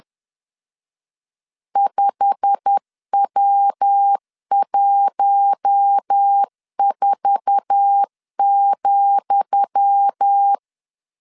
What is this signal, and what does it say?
Morse code, heard as a keyed tone, about 10 wpm, 800 Hz. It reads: 5W14,
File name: LEZIONI / 10 SIGNS / 10D COMA 10wpm 3013